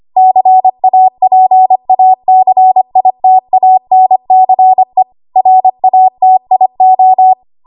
Morse code, heard as a keyed tone, about 25 wpm, 750 Hz. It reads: CAPACITANCE RATIO
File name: Archive / 25 WPM / 25WPM